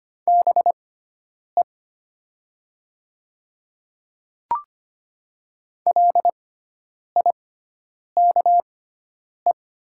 Morse code, heard as a keyed tone, about 25 wpm, spaced as for 9 wpm, 700 Hz. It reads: BE LIKE
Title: 6x Character Spacing - Top 100 Words - 25wpm